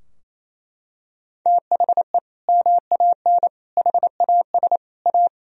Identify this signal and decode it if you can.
Morse code, heard as a keyed tone, about 28 wpm, 700 Hz. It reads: THE MAN HAS A
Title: Sentences from Top 1000 Words - Review Part 3 - 28wpm